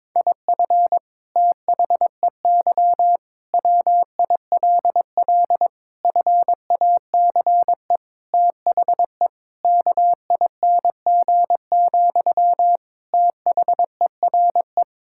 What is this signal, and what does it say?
Morse code, heard as a keyed tone, about 22 wpm, 700 Hz. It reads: IF THEY WILL FACE THE KING, THERE
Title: Sentences from Top 1000 Words - Review Part 6 - 22wpm